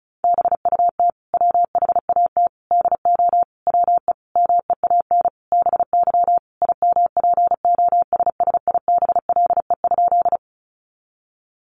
Morse code, heard as a keyed tone, about 34 wpm, 700 Hz. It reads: BUT WHAT DO WE MEAN BY IMPOSSIBLE?
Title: Sentences from Top 1100 Words 35wpm